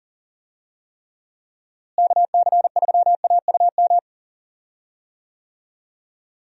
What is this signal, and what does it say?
Morse code, heard as a keyed tone, about 37 wpm, 700 Hz. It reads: KC3AUM
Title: USA Call Signs - 6 Characters - Repeated 2x - 40wpm